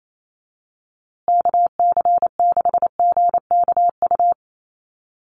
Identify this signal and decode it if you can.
Morse code, heard as a keyed tone, about 28 wpm, 700 Hz. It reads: KC6GKU
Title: USA Call Signs Repeated 3x 28wpm